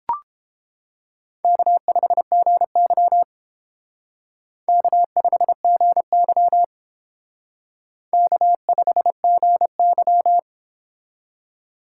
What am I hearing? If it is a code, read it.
Morse code, heard as a keyed tone, about 29 wpm, 700 Hz. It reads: K5GY K5GY K5GY